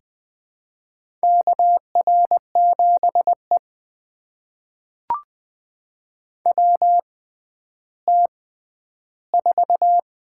Morse code, heard as a keyed tone, about 20 wpm, spaced as for 11 wpm, 700 Hz. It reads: KR7E WT4
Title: USA Call Signs - 4 Characters - 6x Character Spacing - 20wpm